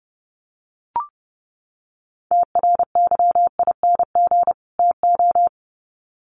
Morse code, heard as a keyed tone, about 30 wpm, 700 Hz. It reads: TRYING TO
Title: Sets of 2 Words 30wpm